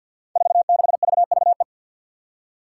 Morse code, heard as a keyed tone, about 51 wpm, 700 Hz. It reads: 46VVE
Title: Five Challenging Letter-Number Combinations 50wpm